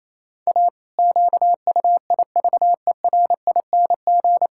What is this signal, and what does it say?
Morse code, heard as a keyed tone, about 27 wpm, 700 Hz. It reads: A QUIVERING
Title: Pangram Challenge v2 - 28wpm